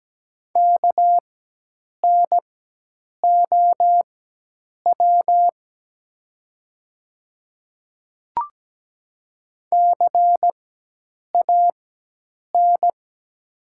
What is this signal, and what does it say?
Morse code, heard as a keyed tone, about 17 wpm, spaced as for 8 wpm, 700 Hz. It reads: KNOW CAN
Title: 4x Character Spacing - Top 100 Words - 17wpm